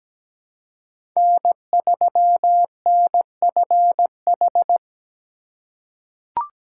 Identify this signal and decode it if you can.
Morse code, heard as a keyed tone, about 17 wpm, 700 Hz. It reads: N3NFH
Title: USA Call Signs - 5 Characters - 2x Character Spacing - 17wpm